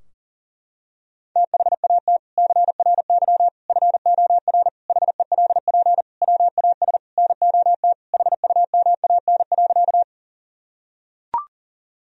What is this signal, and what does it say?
Morse code, heard as a keyed tone, about 40 wpm, 700 Hz. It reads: THAT CRY FOR HELP WAS NOT HUMAN.
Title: Sentences from Top 1000 Words - Review Part 3 - 40wpm